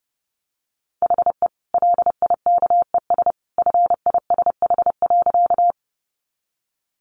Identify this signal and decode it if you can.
Morse code, heard as a keyed tone, about 30 wpm, 700 Hz. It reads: HE LIKES FISH.
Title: Additional Easy Phrases 30wpm